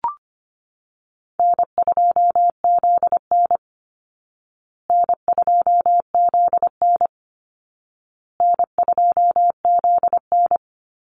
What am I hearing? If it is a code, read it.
Morse code, heard as a keyed tone, about 25 wpm, 700 Hz. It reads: N2ZN N2ZN N2ZN
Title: Easy Call Signs Repeated 3x - 25wpm